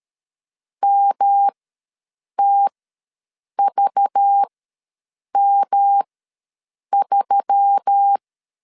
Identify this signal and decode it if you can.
Morse code, heard as a keyed tone, about 13 wpm, spaced as for 7 wpm, 800 Hz. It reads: MTVM3
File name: LEZIONI / 8 YJ45 / 8 YJ ws 12wpm 3342